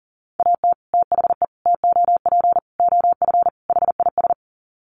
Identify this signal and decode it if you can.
Morse code, heard as a keyed tone, about 40 wpm, 700 Hz. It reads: AT THE TOP OF HIS
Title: Sets of 5 Words Encore 40wpm